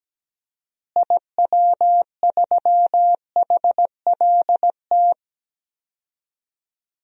Morse code, heard as a keyed tone, about 17 wpm, 700 Hz. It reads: IW3HLT
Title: DXCC-100 Call Signs - 6 Characters - Repeated 3x - 17wpm